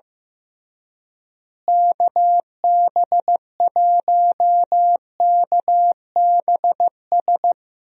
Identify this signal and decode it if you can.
Morse code, heard as a keyed tone, about 15 wpm, 700 Hz. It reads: KB1KBS
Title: USA Call Signs - 6 Characters - Repeated 3x - 15wpm